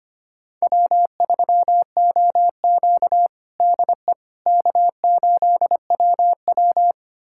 Morse code, heard as a keyed tone, about 25 wpm, 700 Hz. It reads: W3OQ DE K8WW
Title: USA Callsign DE Callsign - 4 Characters - 25wpm